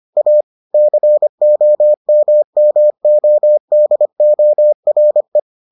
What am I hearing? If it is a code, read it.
Morse code, heard as a keyed tone, about 25 wpm, 600 Hz. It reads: A COMMODORE